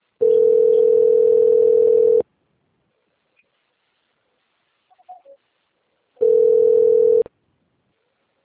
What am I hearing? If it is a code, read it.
Ringback tone (North America), 440 + 480 Hz.